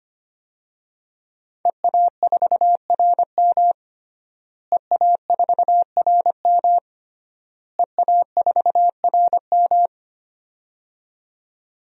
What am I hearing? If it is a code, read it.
Morse code, heard as a keyed tone, about 25 wpm, 700 Hz. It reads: EA4RM EA4RM EA4RM
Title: DXCC-100 Call Signs - 5 Characters - Repeated 3x - 25wpm